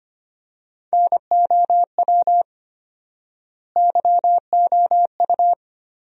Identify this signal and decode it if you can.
Morse code, heard as a keyed tone, about 25 wpm, 700 Hz. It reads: NOW YOU
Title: 4x Word-Spacing - Sentences from Top 100 Words - 25wpm